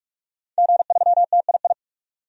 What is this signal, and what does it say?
Morse code, heard as a keyed tone, about 46 wpm, 700 Hz. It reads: G3TII
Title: DXCC-100 Call Signs - 5 Characters - 45wpm